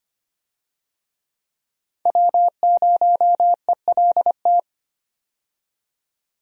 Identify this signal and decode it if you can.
Morse code, heard as a keyed tone, about 25 wpm, 700 Hz. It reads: W0ELT
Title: USA Call Signs - 5 Characters - 25wpm